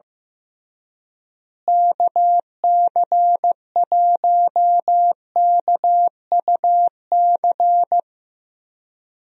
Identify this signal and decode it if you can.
Morse code, heard as a keyed tone, about 15 wpm, 700 Hz. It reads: KC1KUC
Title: USA Call Signs - 6 Characters - Repeated 3x - 15wpm